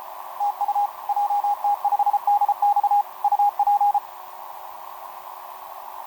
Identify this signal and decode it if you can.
Morse code, heard as a keyed tone, about 31 wpm, 820 Hz, with radio noise. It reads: TU JT5DX UP
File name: jt5dx f5idj etc 12m cw 18mar2011 0915